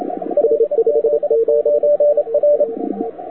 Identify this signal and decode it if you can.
Morse code, heard as a keyed tone, about 27 wpm, 590 Hz. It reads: LZ9R
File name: wwcw08 / k5zd wwcw08 1300z